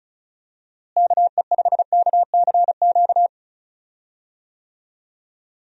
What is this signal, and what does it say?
Morse code, heard as a keyed tone, about 34 wpm, 700 Hz. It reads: KE5KCQ
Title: USA Call Signs - 6 Characters - Repeated 2x - 35wpm